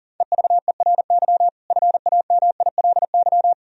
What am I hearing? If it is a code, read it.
Morse code, heard as a keyed tone, about 40 wpm, 700 Hz. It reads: EVERY FAMILY
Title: Sentences from Top 1000 Words - Review Part 1 - 40wpm